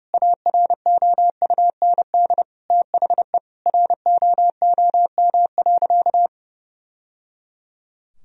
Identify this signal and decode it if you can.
Morse code, heard as a keyed tone, about 30 wpm, 700 Hz. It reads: AROUND THE ROOM.
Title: Sentences from Top 1000 Words - Review Part 3 - 30wpm